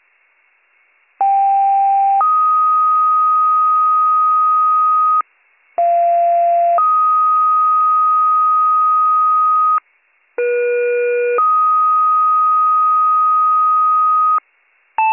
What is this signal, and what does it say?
Two-tone page, heard as a one-second tone, then a three-second tone, about 790 then 1250 Hz, about 690 then 1250 Hz, about 480 then 1250 Hz. It is heard from 1.2 to 14.4 s.